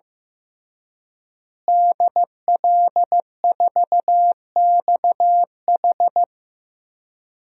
Morse code, heard as a keyed tone, about 15 wpm, 700 Hz. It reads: DL4XH
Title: DXCC-100 Call Signs - 5 Characters - Repeated 3x - 15wpm